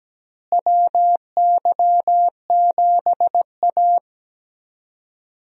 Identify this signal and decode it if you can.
Morse code, heard as a keyed tone, about 17 wpm, 700 Hz. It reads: WY7A
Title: USA Call Signs - 4 Characters - Repeated 3x - 17wpm